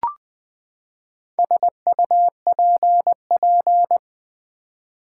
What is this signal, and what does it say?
Morse code, heard as a keyed tone, about 20 wpm, 700 Hz. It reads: SUPP
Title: Build A Word - 4 Letters At A Time - 8-Letter Words - 20wpm